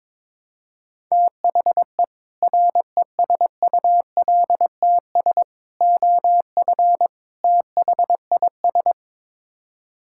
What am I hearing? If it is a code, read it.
Morse code, heard as a keyed tone, about 22 wpm, 700 Hz. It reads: THE RESULTS OF THIS
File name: practice / 22wpm / Build A Phrase - 5-Word Phrases - 22wpm